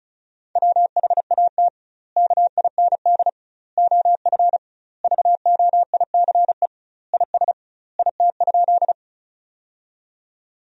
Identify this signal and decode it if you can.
Morse code, heard as a keyed tone, about 36 wpm, 700 Hz. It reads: WHAT KIND OF VOICE IS IT?
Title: 2x Word-Spacing - Sentences from Top 500 Words - 35wpm